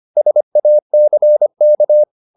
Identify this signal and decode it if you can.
Morse code, heard as a keyed tone, about 25 wpm, 600 Hz. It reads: SACK
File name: the old man and the sea part 1 25wpm